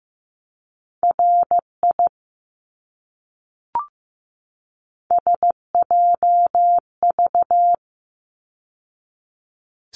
Morse code, heard as a keyed tone, about 15 wpm, 700 Hz. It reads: RI SJV
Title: ARRL Sections 15wpm